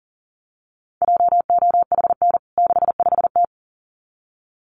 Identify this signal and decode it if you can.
Morse code, heard as a keyed tone, about 40 wpm, 700 Hz. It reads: JOHN 65T
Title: CWT Exchange 40wpm